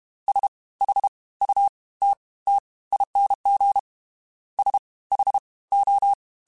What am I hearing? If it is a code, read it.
Morse code, heard as a keyed tone, about 32 wpm, 800 Hz. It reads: S H U T T ING S H O
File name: Special INGs / Special INGs-mp3s / Fast ing @30